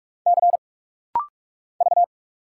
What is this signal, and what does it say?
Morse code, heard as a keyed tone, about 46 wpm, 700 Hz. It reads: C V
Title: Single Letter-Number Alt - Rapid-Fire 45wpm